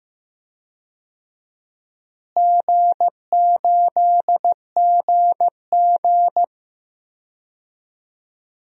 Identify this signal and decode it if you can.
Morse code, heard as a keyed tone, about 15 wpm, 700 Hz. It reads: G8GG
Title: DXCC-100 Call Signs - 4 Characters - Repeated 2x - 15wpm